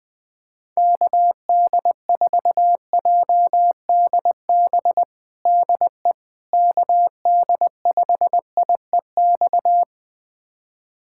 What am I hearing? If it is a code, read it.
Morse code, heard as a keyed tone, about 20 wpm, 700 Hz. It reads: KD4JDB DE KD5IEX
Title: USA Callsign DE Callsign 20wpm